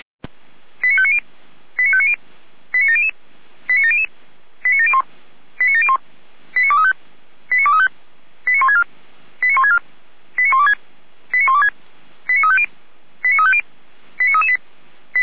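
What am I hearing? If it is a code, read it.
Five-tone selective call (ZVEI), heard as quick runs of five tones, tones of about 70 ms. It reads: 89590 89590 89700 89700 89701 89701 89235 89235 89164 89164 89126 89126 89350 89350 89338
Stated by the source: ZVEI five-tone selective call received by SDR.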